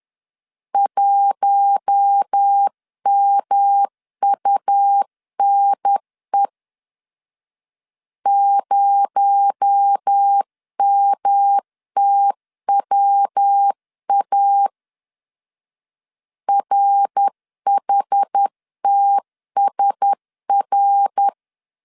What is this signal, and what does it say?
Morse code, heard as a keyed tone, about 10 wpm, 800 Hz. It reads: 1MUNE 0MTWA RHTSR
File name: CW / DET / 1 6 1500C S C 10wpm 4252